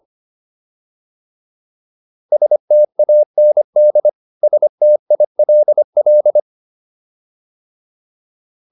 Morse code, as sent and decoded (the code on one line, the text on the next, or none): ... - .- -. -.. / ... - .. .-.. .-..
STAND STILL